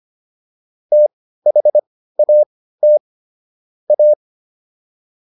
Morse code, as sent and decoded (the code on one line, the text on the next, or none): - .... .- - / .-
THAT A